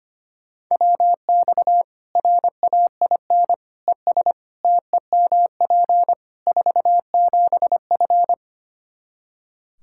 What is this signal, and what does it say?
Morse code, heard as a keyed tone, about 25 wpm, 700 Hz. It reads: WX RAIN ES TEMP 47F
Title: Practice QSO - Exchange 2 - Info - 25wpm